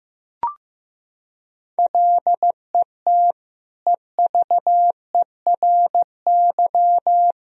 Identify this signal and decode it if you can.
Morse code, heard as a keyed tone, about 15 wpm, 700 Hz. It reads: LET EVERY
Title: Sentences from Top 1000 Words - Review Part 1 - 15wpm